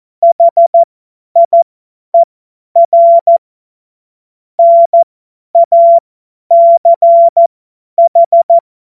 Morse code, heard as a keyed tone, about 14 wpm, spaced as for 10 wpm, 680 Hz. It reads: HIER NACH
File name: Morsekurs / Witze / Witz 15. Februar 2022 50 F70